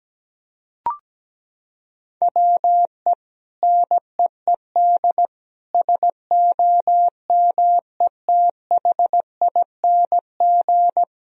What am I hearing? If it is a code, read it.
Morse code, heard as a keyed tone, about 17 wpm, 700 Hz. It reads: WE NEED SOMETHING
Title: Sentences from Top 1000 Words - Review Part 4 - 17wpm